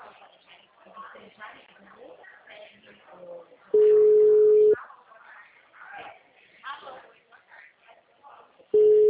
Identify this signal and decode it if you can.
Ringback tone (Europe, 425 Hz), with speech.